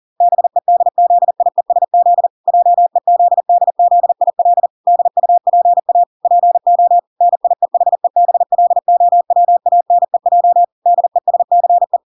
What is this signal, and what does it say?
Morse code, heard as a keyed tone, about 40 wpm, 700 Hz. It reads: BEDZIESZ JEZDZIL DUPA PO NIEHEBLOWANEJ DESCE